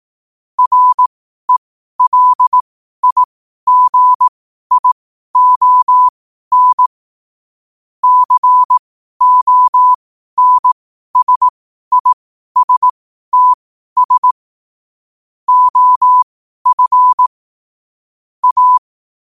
Morse code, heard as a keyed tone, about 18 wpm, spaced as for 12 wpm, 1000 Hz. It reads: RELIGION CONSISTS OF A
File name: qotdfiles / 12wpm